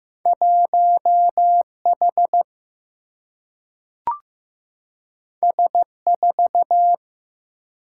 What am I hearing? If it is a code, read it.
Morse code, heard as a keyed tone, about 15 wpm, 700 Hz. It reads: 1 H S 4
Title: Single-Letter Single-Number 15wpm